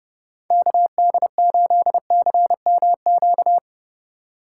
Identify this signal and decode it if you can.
Morse code, heard as a keyed tone, about 31 wpm, 700 Hz. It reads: KD8CMQ